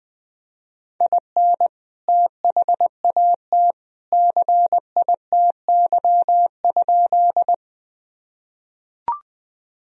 Morse code, heard as a keyed tone, about 21 wpm, 700 Hz. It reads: IN THAT CITY?
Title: Sentences from Top 1000 Words - Review Part 1 - 20wpm